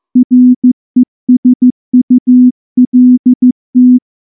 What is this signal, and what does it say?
Morse code, heard as a keyed tone, about 14 wpm, 250 Hz. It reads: RESULT
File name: morse / morse-training-15WPM